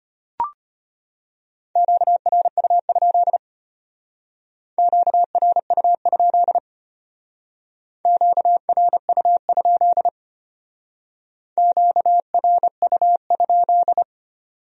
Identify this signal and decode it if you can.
Morse code, heard as a keyed tone, about 33 wpm, 700 Hz. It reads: QRU? QRU? QRU? QRU?